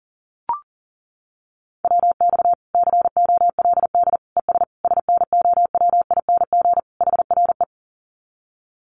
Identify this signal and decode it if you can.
Morse code, heard as a keyed tone, about 40 wpm, 700 Hz. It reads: WX COLD ES SNOWING HRE
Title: Common QSO Phrases 40wpm